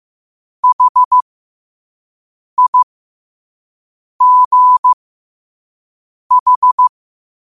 Morse code, heard as a keyed tone, about 15 wpm, spaced as for 5 wpm, 1000 Hz. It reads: HIGH
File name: qotdfiles / 5wpm